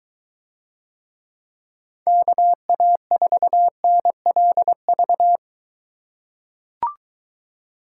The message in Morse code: KA4NLV

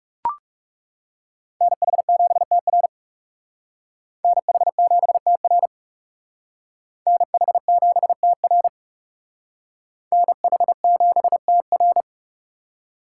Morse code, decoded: NH7TR NH7TR NH7TR NH7TR

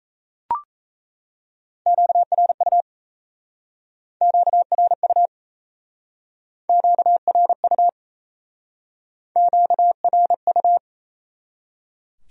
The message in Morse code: QRU QRU QRU QRU

